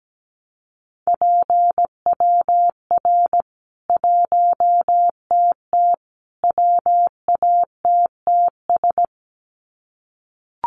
Morse code, heard as a keyed tone, about 17 wpm, 700 Hz. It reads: PWR 1TT WATTS
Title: Common QSO Phrases 17wpm